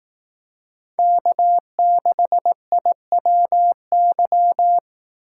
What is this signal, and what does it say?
Morse code, heard as a keyed tone, about 18 wpm, 700 Hz. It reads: K6IWY
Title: Speed Racing - USA 1x3 Call Signs 15wpm